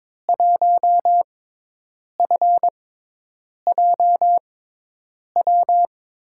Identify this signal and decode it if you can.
Morse code, heard as a keyed tone, about 22 wpm, spaced as for 8 wpm, 700 Hz. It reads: 1FJW
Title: USA Call Signs - 6 Characters - 6x Character Spacing - 22wpm